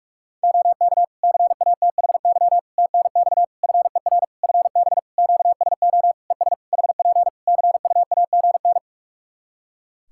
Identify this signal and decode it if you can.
Morse code, heard as a keyed tone, about 46 wpm, 700 Hz. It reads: OK CATHY TNX FER FB QSO ES HP CUAGN
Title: Practice QSO - Exchange 4 - Ending - 45wpm